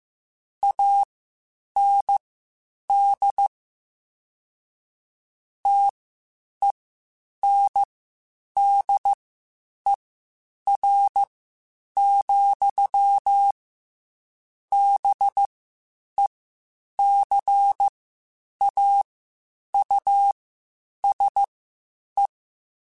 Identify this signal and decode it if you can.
Morse code, heard as a keyed tone, about 15 wpm, spaced as for 8 wpm, 800 Hz. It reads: AND TENDER, BECAUSE
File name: Spaced / Chuckles Spaced-mp3s / SoftWords sp @13